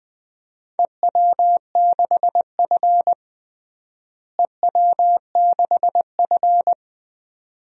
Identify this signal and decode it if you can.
Morse code, heard as a keyed tone, about 21 wpm, 700 Hz. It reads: EW6F EW6F
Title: DXCC-100 Call Signs - 4 Characters - Repeated 2x - 20wpm